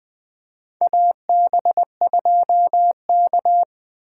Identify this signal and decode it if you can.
Morse code, heard as a keyed tone, about 20 wpm, 700 Hz. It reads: AB2K
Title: USA Call Signs - 4 Characters - 3x Character Spacing - 20wpm